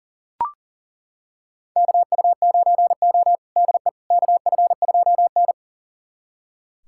Morse code, heard as a keyed tone, about 40 wpm, 700 Hz. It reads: KU9O DE KF2N
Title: USA Callsign DE Callsign - 4 Characters - 40wpm